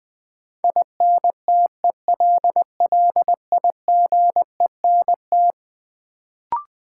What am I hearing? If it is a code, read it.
Morse code, heard as a keyed tone, about 20 wpm, 700 Hz. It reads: INTELLIGENT
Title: Sentences from Top 1000 Words - Review Part 7 - New Word Drill - 20wpm